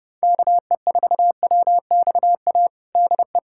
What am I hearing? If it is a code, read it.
Morse code, heard as a keyed tone, about 30 wpm, 700 Hz. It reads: KE4WXA DE
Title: USA Callsign DE Callsign - 6 Characters - 30wpm